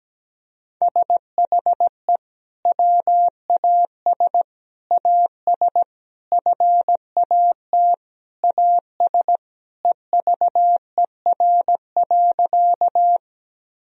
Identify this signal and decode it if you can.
Morse code, heard as a keyed tone, about 17 wpm, 700 Hz. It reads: SHE WAS AS FAT AS EVER.
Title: Sentences from Top 1000 Words - Review Part 4 - 17wpm